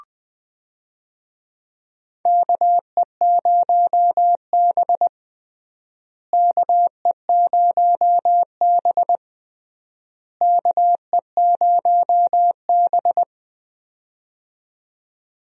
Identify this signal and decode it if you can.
Morse code, heard as a keyed tone, about 20 wpm, 700 Hz. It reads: KE0B KE0B KE0B
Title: USA Call Signs - 4 Characters - Repeated 3x - 20wpm